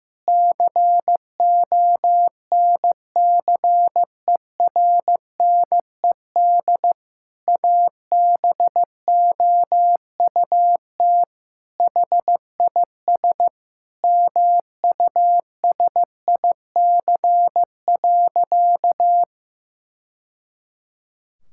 Morse code, heard as a keyed tone, about 15 wpm, 700 Hz. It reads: CONCERNED ABOUT HIS MUSIC.
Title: Sentences from Top 1000 Words - Review Part 6 - 15wpm